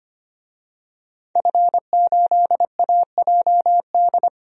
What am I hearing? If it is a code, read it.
Morse code, heard as a keyed tone, about 25 wpm, 700 Hz. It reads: F8AJD
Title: DXCC-100 Call Signs - 5 Characters - 25wpm